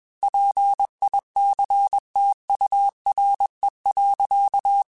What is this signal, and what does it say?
Morse code, heard as a keyed tone, about 21 wpm, 800 Hz. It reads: PICTURE.
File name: Technical Subjects / Technical Subjects-mp3s / AND Gate @20